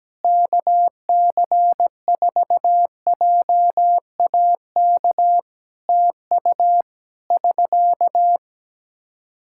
Morse code, heard as a keyed tone, about 17 wpm, 700 Hz. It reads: KC4JAK TU <SK>
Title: K1USN SST QSO Practice 17wpm